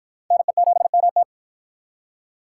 Morse code, heard as a keyed tone, about 52 wpm, 700 Hz. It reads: NE7MT